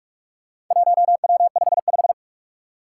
Morse code, heard as a keyed tone, about 43 wpm, 700 Hz. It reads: 1W55
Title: Four Challenging Letter-Number Combinations 45wpm